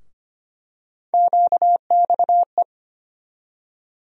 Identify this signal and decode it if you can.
Morse code, heard as a keyed tone, about 25 wpm, 700 Hz. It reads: QXE
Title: USA Call Sign Postfixes - 3 Letters 25wpm